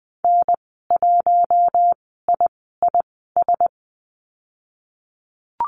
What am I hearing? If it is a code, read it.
Morse code, heard as a keyed tone, about 20 wpm, spaced as for 15 wpm, 700 Hz. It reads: N1IIS
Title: 048 Lesson 9g - Number 4 - Call Signs 20wpm